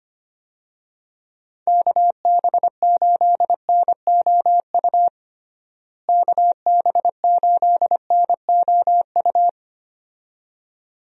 Morse code, heard as a keyed tone, about 25 wpm, 700 Hz. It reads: KB8NOU KB8NOU
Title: USA Call Signs - 6 Characters - Repeated 2x - 25wpm